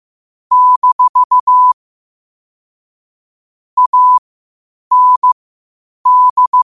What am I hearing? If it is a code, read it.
Morse code, heard as a keyed tone, about 15 wpm, spaced as for 8 wpm, 1000 Hz. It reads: - AND
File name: qotdfiles / 8wpm